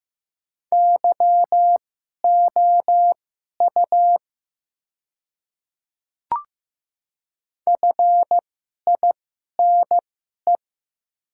Morse code, heard as a keyed tone, about 15 wpm, spaced as for 11 wpm, 700 Hz. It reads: YOU FINE
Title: 2x Character Spacing - Top 500 Words - 15wpm